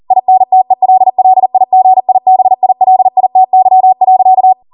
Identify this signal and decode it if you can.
Morse code, heard as a keyed tone, about 40 wpm, 750 Hz. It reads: INTELLIGIBILITY.